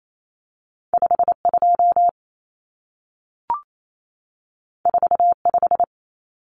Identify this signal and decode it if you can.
Morse code, heard as a keyed tone, about 28 wpm, 700 Hz. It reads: 52 45
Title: Numbers - 2-Digit 28wpm